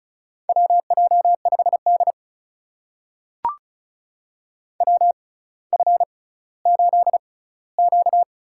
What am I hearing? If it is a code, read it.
Morse code, heard as a keyed tone, about 36 wpm, spaced as for 18 wpm, 700 Hz. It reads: WJ5D WF8Q
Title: USA Call Signs - 4 Characters - 6x Character Spacing - 35wpm